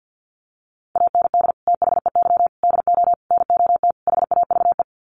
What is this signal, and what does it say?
Morse code, heard as a keyed tone, about 50 wpm, 700 Hz. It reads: AND THEY DO NOT HAVE
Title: Sets of 5 Words Encore2 50wpm